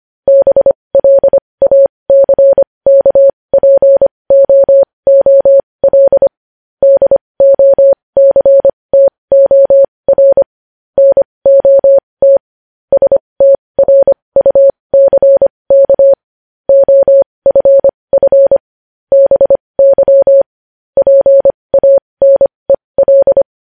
Morse code, heard as a keyed tone, about 25 wpm, spaced as for 20 wpm, 550 Hz. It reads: BLACKPOOL DOCTOR NOT STRUCK OFF BY PANEL
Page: News Headlines in Morse Code at 20 WPM